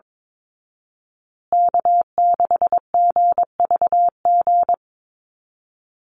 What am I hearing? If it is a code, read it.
Morse code, heard as a keyed tone, about 22 wpm, 700 Hz. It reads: K6GVG
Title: Call Signs Repeated 3x - 22wpm